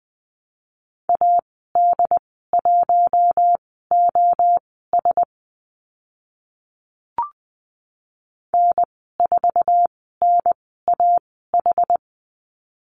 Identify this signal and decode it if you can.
Morse code, heard as a keyed tone, about 20 wpm, spaced as for 14 wpm, 700 Hz. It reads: AD1OS N4NAH
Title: 069 Lesson 12g - Letter d - Call Signs 20wpm